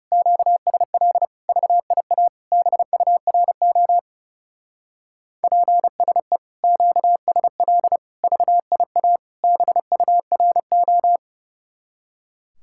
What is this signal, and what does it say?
Morse code, heard as a keyed tone, about 36 wpm, 700 Hz. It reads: QSL VIA BURO PSE QSL VIA BURO
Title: Speed-Racing Common QSO Phrases - 30wpm